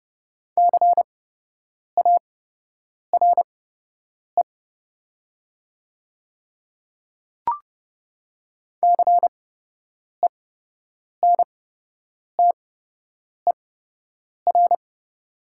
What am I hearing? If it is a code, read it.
Morse code, heard as a keyed tone, about 30 wpm, spaced as for 8 wpm, 690 Hz. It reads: CARE CENTER
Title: 8x Character Spacing - Top 500 Words - 30wpm